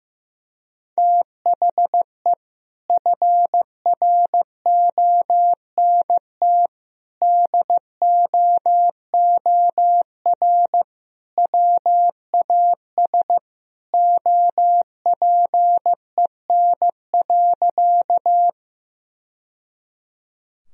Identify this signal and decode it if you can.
Morse code, heard as a keyed tone, about 16 wpm, 700 Hz. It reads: THE FRONT DOOR WAS OPEN.
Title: Sentences from Top 1000 Words - Review Part 2 - 15wpm